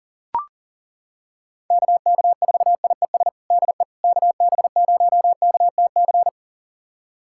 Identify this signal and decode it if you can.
Morse code, heard as a keyed tone, about 37 wpm, 700 Hz. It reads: KK4IES DE KB0KTC
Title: USA Callsign DE Callsign 40wpm